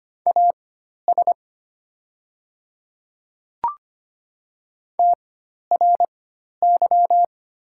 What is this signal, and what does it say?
Morse code, heard as a keyed tone, about 25 wpm, spaced as for 12 wpm, 700 Hz. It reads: AS TRY
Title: 4x Character Spacing - Top 200 Words - 25wpm